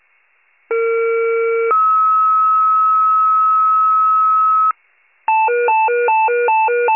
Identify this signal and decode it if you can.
Two-tone page, heard as a one-second tone, then a three-second tone, about 460 then 1290 Hz. It is heard from 0.7 to 4.7 s.
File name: pager / 21 RESCUE / wcalert